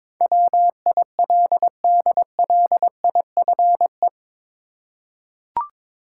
Morse code, heard as a keyed tone, about 22 wpm, 700 Hz. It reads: WILDLIFE
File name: practice / 22wpm / Eight Letter Words - 22wpm